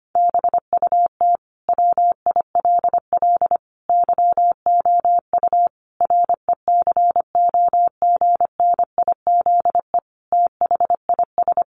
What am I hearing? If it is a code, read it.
Morse code, heard as a keyed tone, about 25 wpm, 700 Hz. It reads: BUT WILL YOU RECOGNIZE THIS
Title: Sentences from Top 1400 Words 25wpm